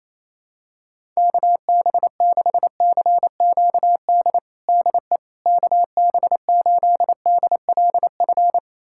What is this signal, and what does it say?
Morse code, heard as a keyed tone, about 28 wpm, 700 Hz. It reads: KB6CQD DE KB8DLF
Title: USA Callsign DE Callsign 28wpm